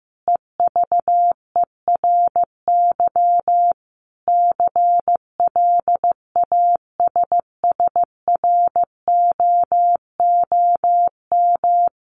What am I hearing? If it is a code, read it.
Morse code, heard as a keyed tone, about 15 wpm, 700 Hz. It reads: EVERY CLASSROOM
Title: Sentences from Top 1500 Words 15wpm